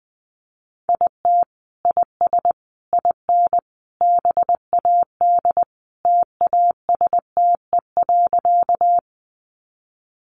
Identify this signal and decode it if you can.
Morse code, heard as a keyed tone, about 20 wpm, 700 Hz. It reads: IT IS IN BAD TASTE.
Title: Sentences from Top 1100 Words 20wpm